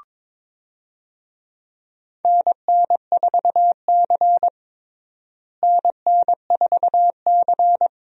Morse code, heard as a keyed tone, about 22 wpm, 700 Hz. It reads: NN4C NN4C
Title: USA Call Signs - 4 Characters - Repeated 3x - 22wpm